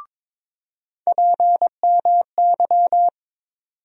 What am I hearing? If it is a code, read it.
Morse code, heard as a keyed tone, about 22 wpm, 700 Hz. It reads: PMY
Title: USA Call Sign Postfixes - 3 Letters 22wpm